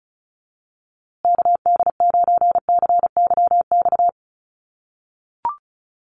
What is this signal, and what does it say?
Morse code, heard as a keyed tone, about 34 wpm, 700 Hz. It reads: KD9CYX
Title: USA Call Signs Repeated 3x 35wpm